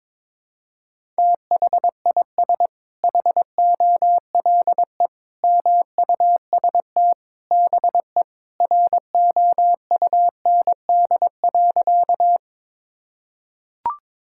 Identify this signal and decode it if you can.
Morse code, heard as a keyed tone, about 22 wpm, 700 Hz. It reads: THIS HOLE MUST BE ROUND.
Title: Sentences from Top 1600 Words 22wpm